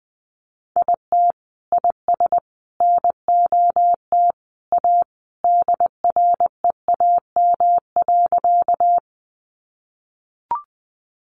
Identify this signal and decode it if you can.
Morse code, heard as a keyed tone, about 20 wpm, 700 Hz. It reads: IT IS NOT A DREAM.
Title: Sentences from Top 1500 Words 20wpm